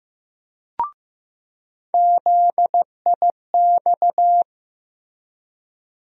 Morse code, heard as a keyed tone, about 15 wpm, 700 Hz. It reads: ZIX
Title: USA Call Sign Postfixes - 3 Letters 15wpm